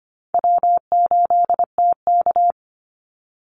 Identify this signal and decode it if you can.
Morse code, heard as a keyed tone, about 25 wpm, 700 Hz. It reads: W8TK
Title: Easy Call Signs - 25wpm